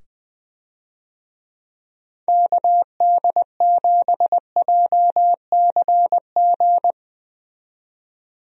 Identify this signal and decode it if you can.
Morse code, heard as a keyed tone, about 20 wpm, 700 Hz. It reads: KD7JCG
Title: USA Call Signs - 6 Characters - 20wpm